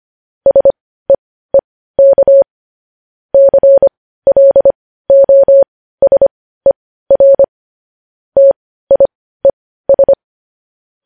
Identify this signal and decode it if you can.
Morse code, heard as a keyed tone, about 25 wpm, spaced as for 15 wpm, 550 Hz. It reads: SEEK CLOSER TIES
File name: podcast / News headlines-1x-25x15